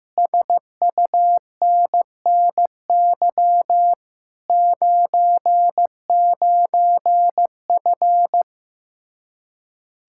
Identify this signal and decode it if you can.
Morse code, heard as a keyed tone, about 15 wpm, 700 Hz. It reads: SUNNY 99F
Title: Speed-Racing Common QSO Phrases Encore - 15wpm